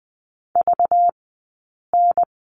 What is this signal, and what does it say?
Morse code, heard as a keyed tone, about 20 wpm, spaced as for 8 wpm, 700 Hz. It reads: VN
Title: 204 Lesson 33d - Letter j - Two Characters 20wpm